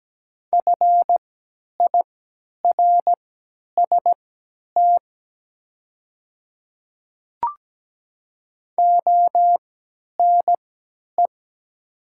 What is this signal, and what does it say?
Morse code, heard as a keyed tone, about 17 wpm, spaced as for 10 wpm, 700 Hz. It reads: FIRST ONE
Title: 3x Character Spacing - Top 100 Words - 17wpm